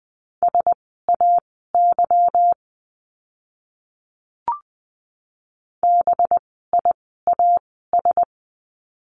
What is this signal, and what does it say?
Morse code, heard as a keyed tone, about 20 wpm, spaced as for 14 wpm, 700 Hz. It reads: SAY BIAS
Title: 185 Lesson 30f - Letter b - Words 20wpm